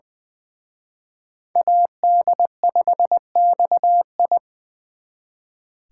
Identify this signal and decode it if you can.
Morse code, heard as a keyed tone, about 24 wpm, 700 Hz. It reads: AD5XI
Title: Speed Racing - USA 2x2 Call Signs 20wpm